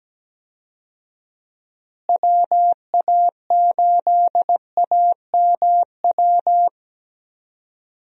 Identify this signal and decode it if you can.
Morse code, heard as a keyed tone, about 17 wpm, 700 Hz. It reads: WA8AMW
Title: USA Call Signs - 6 Characters - Repeated 3x - 17wpm